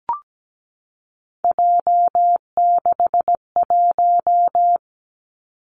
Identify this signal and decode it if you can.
Morse code, heard as a keyed tone, about 17 wpm, 700 Hz. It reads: J61
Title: Three Challenging Letter-Number Combinations 17wpm